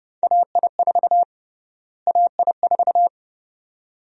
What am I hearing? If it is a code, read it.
Morse code, heard as a keyed tone, about 29 wpm, 700 Hz. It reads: AI4 AI4
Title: USA Call Sign Prefixes - Repeated 2x - 30wpm